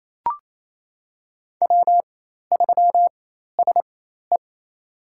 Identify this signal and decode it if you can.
Morse code, heard as a keyed tone, about 28 wpm, spaced as for 13 wpm, 700 Hz. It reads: W3SE
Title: USA Call Signs - 4 Characters - 4x Character Spacing - 28wpm